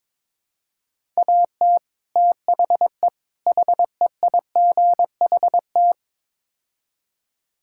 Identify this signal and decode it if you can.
Morse code, heard as a keyed tone, about 22 wpm, 700 Hz. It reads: AT THE HEIGHT
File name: practice / 22wpm / Build A Phrase - 5-Word Phrases - 22wpm